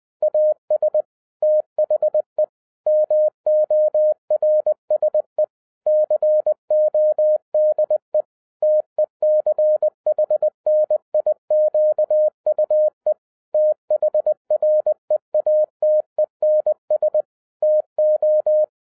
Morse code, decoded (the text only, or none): AS THE MORSE CODE TECHNIQUE THREATENS TO